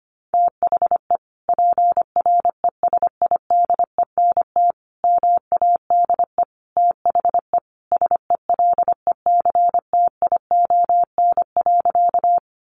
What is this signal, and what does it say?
Morse code, heard as a keyed tone, about 25 wpm, 700 Hz. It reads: THE PRESIDENT MADE THE SELECTION.
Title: Sentences from Top 1400 Words 25wpm